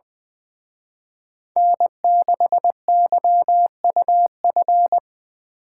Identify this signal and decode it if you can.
Morse code, heard as a keyed tone, about 23 wpm, 700 Hz. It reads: N6YUF
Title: Speed Racing - USA 1x3 Call Signs 20wpm